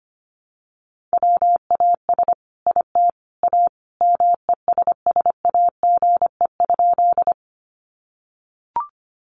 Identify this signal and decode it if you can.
Morse code, heard as a keyed tone, about 25 wpm, 700 Hz. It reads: WAS IT A MESSAGE?
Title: Sentences from Top 1100 Words 25wpm